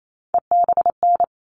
Morse code, heard as a keyed tone, about 28 wpm, 700 Hz. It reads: EBN
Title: All Cut Numbers - 3-Digit 28wpm